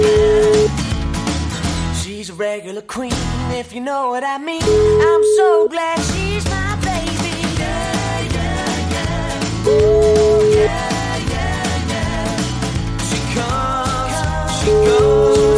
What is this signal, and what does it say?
Ringback tone (Europe, 425 Hz), with noise.